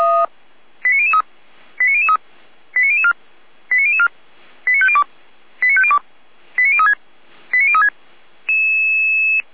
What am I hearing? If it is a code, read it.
Five-tone selective call (ZVEI), heard as quick runs of five tones, tones of about 70 ms. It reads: 89003 89003 89004 89004 89502 89502 89036 89036